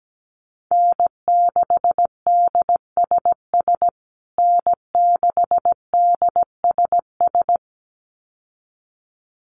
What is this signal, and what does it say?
Morse code, heard as a keyed tone, about 17 wpm, 700 Hz. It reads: N6DSS N6DSS
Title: USA Call Signs Repeated 2x 17wpm